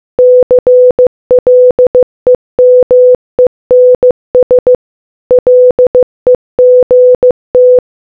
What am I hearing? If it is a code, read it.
Morse code, heard as a keyed tone, about 15 wpm, 500 Hz. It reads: CLEMENS LEGT